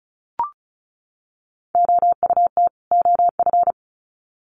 Morse code, heard as a keyed tone, about 34 wpm, 700 Hz. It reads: OUT OF
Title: Sets of 2 Words 35wpm